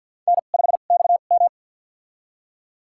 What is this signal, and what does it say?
Morse code, heard as a keyed tone, about 46 wpm, spaced as for 35 wpm, 700 Hz. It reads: N5XM
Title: USA Call Signs - 2x Character Spacing - 50wpm